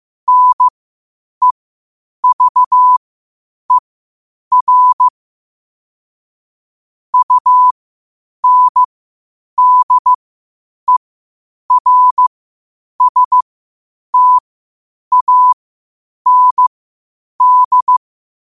Morse code, heard as a keyed tone, about 15 wpm, spaced as for 8 wpm, 1000 Hz. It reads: NEVER UNDERSTAND
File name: qotdfiles / 8wpm